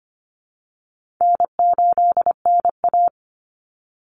Morse code, heard as a keyed tone, about 25 wpm, 700 Hz. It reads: N8NA
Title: Call Signs Repeated 2x - 25wpm